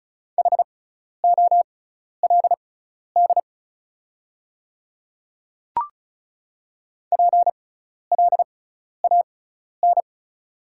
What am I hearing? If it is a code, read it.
Morse code, heard as a keyed tone, about 34 wpm, spaced as for 12 wpm, 700 Hz. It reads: HOLD PLAN
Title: 6x Character Spacing - Top 500 Words - 35wpm